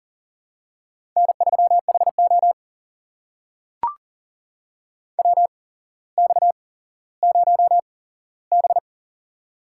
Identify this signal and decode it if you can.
Morse code, heard as a keyed tone, about 40 wpm, spaced as for 17 wpm, 700 Hz. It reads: N3HO WX0B
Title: USA Call Signs - 4 Characters - 8x Character Spacing - 40wpm